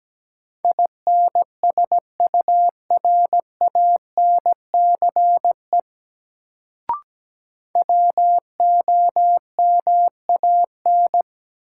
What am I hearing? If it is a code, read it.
Morse code, heard as a keyed tone, about 17 wpm, 700 Hz. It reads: INSURANCE WOMAN
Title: Sentences from Top 1100 Words - New Word Drill Alt - 17wpm